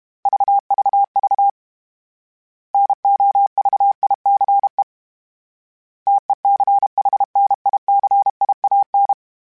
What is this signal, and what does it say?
Morse code, heard as a keyed tone, about 32 wpm, 800 Hz. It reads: VVV NOVICE TECHNICIAN